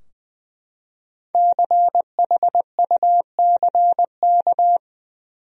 Morse code, heard as a keyed tone, about 20 wpm, 700 Hz. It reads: CHUCK
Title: USA Common Names - Max 5 Letters - 20wpm